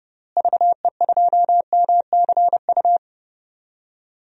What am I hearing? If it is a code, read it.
Morse code, heard as a keyed tone, about 30 wpm, 700 Hz. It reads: VE2MCU
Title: DXCC-100 Call Signs - 6 Characters - Repeated 3x - 30wpm